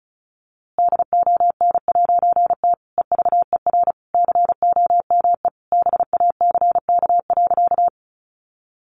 Morse code, heard as a keyed tone, about 34 wpm, 700 Hz. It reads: DON'T EVER COME BACK.
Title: A Sound Of Thunder 35wpm